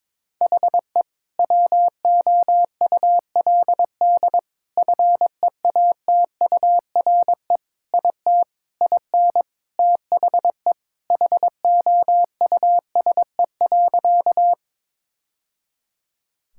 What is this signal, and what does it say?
Morse code, heard as a keyed tone, about 21 wpm, 700 Hz. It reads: HE WOULD FEATURE IT IN THE HOUSE.
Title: Sentences from Top 1000 Words - Review Part 5 - 22wpm